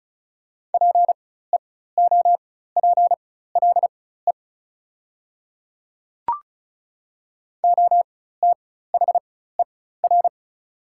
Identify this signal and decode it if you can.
Morse code, heard as a keyed tone, about 34 wpm, spaced as for 16 wpm, 700 Hz. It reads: PEOPLE OTHER
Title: 4x Character Spacing - Top 200 Words - 35wpm